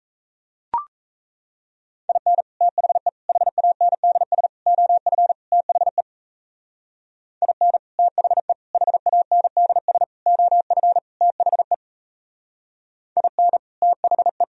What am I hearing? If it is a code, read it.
Morse code, heard as a keyed tone, about 34 wpm, 700 Hz. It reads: IN THE HANDS OF THE IN THE HANDS OF THE IN THE
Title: Speed-Racing - Sets of 5 Words 28wpm